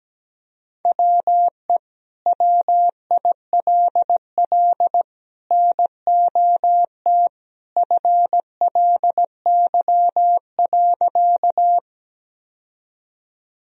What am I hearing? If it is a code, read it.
Morse code, heard as a keyed tone, about 17 wpm, 700 Hz. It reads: WE WILL NOT FLY.
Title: Sentences from Top 1000 Words - Review Part 2 - 17wpm